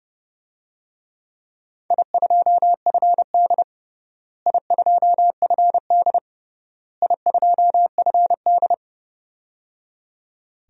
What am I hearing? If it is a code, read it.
Morse code, heard as a keyed tone, about 30 wpm, 700 Hz. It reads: I2FD I2FD I2FD
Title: DXCC-100 Call Signs - 4 Characters - Repeated 3x - 30wpm